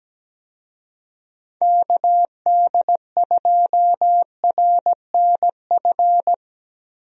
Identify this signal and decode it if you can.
Morse code, heard as a keyed tone, about 17 wpm, 700 Hz. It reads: KD2RNF
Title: USA Call Signs - 6 Characters - 17wpm